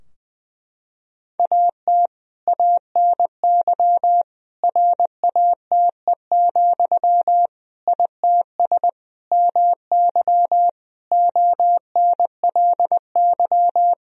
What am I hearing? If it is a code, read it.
Morse code, heard as a keyed tone, about 20 wpm, 700 Hz. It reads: AT ANY RATE, ITS MY ONLY
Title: Sentences from Top 1000 Words - Review Part 4 - 20wpm